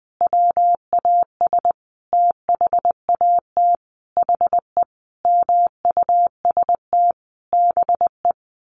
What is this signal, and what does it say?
Morse code, heard as a keyed tone, about 20 wpm, 700 Hz. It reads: WAS THAT HE MUST BE
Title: Sentences from Top 1500 Words 20wpm